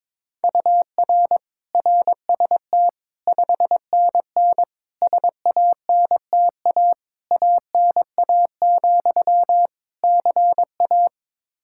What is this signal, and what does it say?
Morse code, heard as a keyed tone, about 21 wpm, 700 Hz. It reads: UR RST 5NN SANTA ANA, CA
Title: Practice QSO - Exchange 1 - Beginning - 22wpm